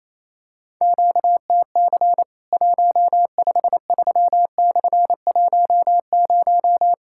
Morse code, heard as a keyed tone, about 28 wpm, 700 Hz. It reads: QTC 153/10
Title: Worked All Europe DX Contest - QTC Exchange - 28wpm